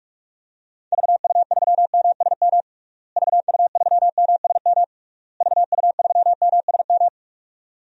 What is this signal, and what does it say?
Morse code, heard as a keyed tone, about 46 wpm, 700 Hz. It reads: VU3MSM VU3MSM VU3MSM
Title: DXCC-100 Call Signs - 6 Characters - Repeated 3x - 45wpm